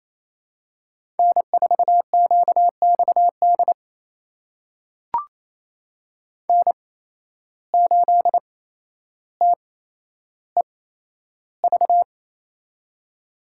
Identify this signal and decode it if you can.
Morse code, heard as a keyed tone, about 28 wpm, spaced as for 12 wpm, 700 Hz. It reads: N4QXD N8TEV
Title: USA Call Signs - 5 Characters - 8x Character Spacing - 28wpm